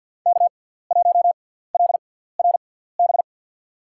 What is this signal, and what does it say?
Morse code, heard as a keyed tone, about 47 wpm, spaced as for 17 wpm, 700 Hz. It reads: K1LRB
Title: USA Call Signs - 5 Characters - 6x Character Spacing - 50wpm